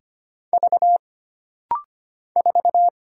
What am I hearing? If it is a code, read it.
Morse code, heard as a keyed tone, about 25 wpm, 700 Hz. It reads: V 4
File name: Single Character (Letter:Number:Punctuation:Prosign) - Rapid-Fire - 25wpm